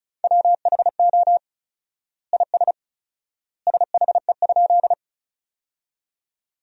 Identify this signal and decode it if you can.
Morse code, heard as a keyed tone, about 34 wpm, 700 Hz. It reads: WHO IS SHE?
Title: 4x Word-Spacing - Sentences from Top 100 Words - 35wpm